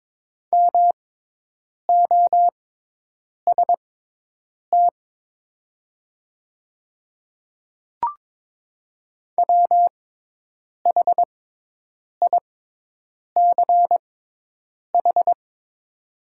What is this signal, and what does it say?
Morse code, heard as a keyed tone, about 22 wpm, spaced as for 8 wpm, 700 Hz. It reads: MOST WHICH
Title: 6x Character Spacing - Top 100 Words - 22wpm